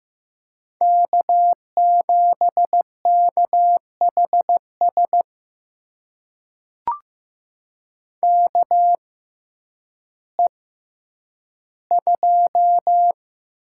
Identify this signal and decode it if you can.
Morse code, heard as a keyed tone, about 15 wpm, spaced as for 9 wpm, 700 Hz. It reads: K7KHS KE2
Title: USA Call Signs - 5 Characters - 6x Character Spacing - 15wpm